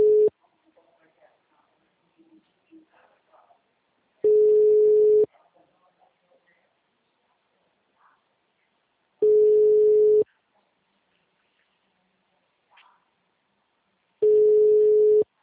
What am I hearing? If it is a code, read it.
Ringback tone (Europe, 425 Hz), with speech.